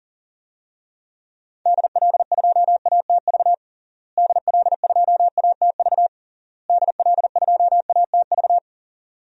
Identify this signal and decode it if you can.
Morse code, heard as a keyed tone, about 40 wpm, 700 Hz. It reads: DL2ATV DL2ATV DL2ATV
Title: DXCC-100 Call Signs - 6 Characters - Repeated 3x - 40wpm